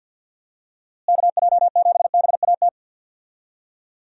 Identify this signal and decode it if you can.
Morse code, heard as a keyed tone, about 50 wpm, 700 Hz. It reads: KJ7BAT
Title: USA Call Signs - 6 Characters - Repeated 3x - 50wpm